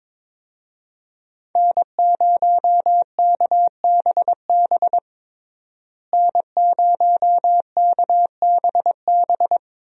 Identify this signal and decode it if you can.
Morse code, heard as a keyed tone, about 22 wpm, 700 Hz. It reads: N0KBB N0KBB
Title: USA Call Signs - 5 Characters - Repeated 2x - 22wpm